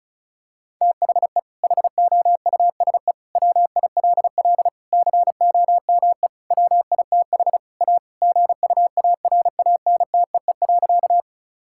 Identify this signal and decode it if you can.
Morse code, heard as a keyed tone, about 36 wpm, 700 Hz. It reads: THE HOUSE WILL COME WITH A GUARANTEE.
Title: Sentences from Top 1600 Words 35wpm